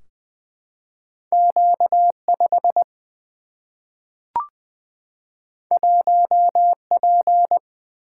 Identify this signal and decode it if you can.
Morse code, heard as a keyed tone, about 20 wpm, 700 Hz. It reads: Q5 1P
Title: Single-Letter Single-Number 20wpm